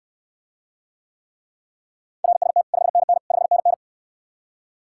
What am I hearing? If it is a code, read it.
Morse code, heard as a keyed tone, about 69 wpm, 700 Hz. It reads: RST 5NN 5NN